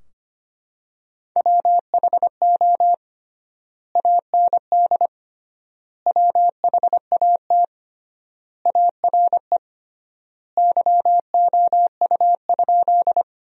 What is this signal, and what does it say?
Morse code, heard as a keyed tone, about 25 wpm, 700 Hz. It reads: WHO AND WHAT ARE YOU?
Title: 3x Word-Spacing - Sentences from Top 100 Words - 25wpm